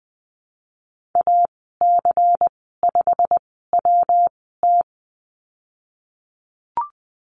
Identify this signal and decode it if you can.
Morse code, heard as a keyed tone, about 20 wpm, spaced as for 15 wpm, 700 Hz. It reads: AC5WT
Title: 122 Lesson 20g - Number 3 - Call Signs 20wpm